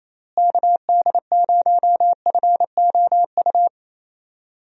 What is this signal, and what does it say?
Morse code, heard as a keyed tone, about 28 wpm, 700 Hz. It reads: KD0FOU
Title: USA Call Signs - 6 Characters - Repeated 2x - 28wpm